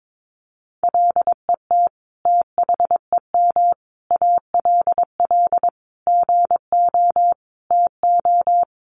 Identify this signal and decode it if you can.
Morse code, heard as a keyed tone, about 22 wpm, 700 Hz. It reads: LET THEM ALL GO TO
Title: Sentences from Top 1000 Words 22wpm